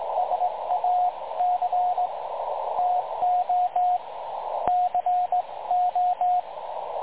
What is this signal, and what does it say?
Morse code, heard as a keyed tone, about 18 wpm, 720 Hz, with radio noise. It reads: SAC TO CO
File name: dlazaawansowanych / nagraniazpasma / websdr recording 3565.0kHz